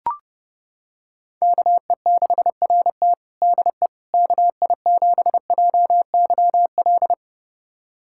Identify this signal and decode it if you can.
Morse code, heard as a keyed tone, about 30 wpm, 700 Hz. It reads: KE6RT DE KI7JYL
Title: USA Callsign DE Callsign 30wpm